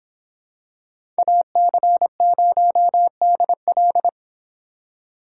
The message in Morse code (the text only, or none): AC0DL